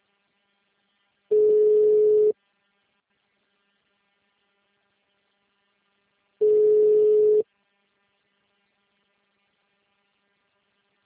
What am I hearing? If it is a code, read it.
Ringback tone (Europe, 425 Hz).
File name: out- -2024- - - .0